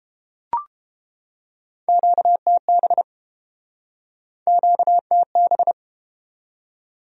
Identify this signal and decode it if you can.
Morse code, heard as a keyed tone, about 30 wpm, 700 Hz. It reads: QTB QTB